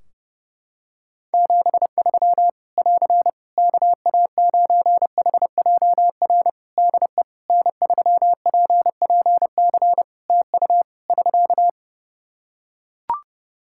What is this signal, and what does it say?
Morse code, heard as a keyed tone, about 30 wpm, 700 Hz. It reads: 73 <AR> KA9HJR DE N3PPC TU <SK>
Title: Practice QSO - Exchange 4 - Ending - 30wpm